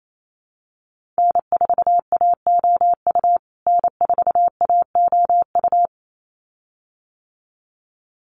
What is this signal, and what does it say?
Morse code, heard as a keyed tone, about 28 wpm, 700 Hz. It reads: N4AOU N4AOU
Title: USA Call Signs Repeated 2x 28wpm